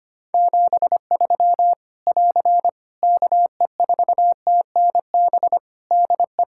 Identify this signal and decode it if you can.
Morse code, heard as a keyed tone, about 25 wpm, 700 Hz. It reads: 73 <AR> KE4TNB DE
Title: K1USN SST QSO Practice 25wpm